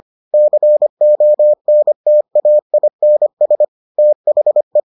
Morse code, heard as a keyed tone, about 25 wpm, 600 Hz. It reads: CONTAINS THE